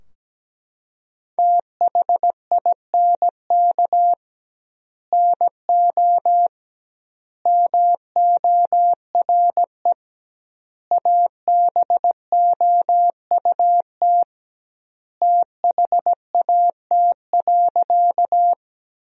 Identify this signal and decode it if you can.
Morse code, heard as a keyed tone, about 17 wpm, 700 Hz. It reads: THINK NO MORE ABOUT THAT.
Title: 2x Word-Spacing - Sentences from Top 200 Words - 17wpm